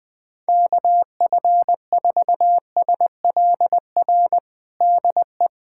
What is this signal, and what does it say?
Morse code, heard as a keyed tone, about 21 wpm, 700 Hz. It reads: KF4SLR DE
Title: USA Callsign DE Callsign - 6 Characters - 20wpm